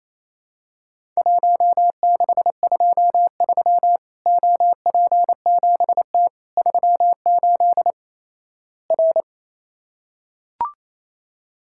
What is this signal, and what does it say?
Morse code, heard as a keyed tone, about 28 wpm, 700 Hz. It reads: 1623 OP7T 38 R
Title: Worked All Europe DX Contest - QTC Exchange - 28wpm